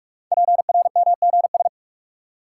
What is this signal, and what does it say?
Morse code, heard as a keyed tone, about 44 wpm, 700 Hz. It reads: PRMGS
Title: Code Groups - 5 Letters - 45wpm